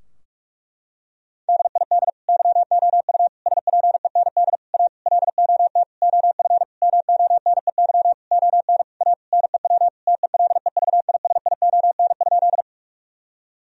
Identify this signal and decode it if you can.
Morse code, heard as a keyed tone, about 46 wpm, 700 Hz. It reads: DID YOU SPEND A LOT OF MONEY ON A NEW TELEVISION?
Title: Sentences from Top 1000 Words - Review Part 4 - 45wpm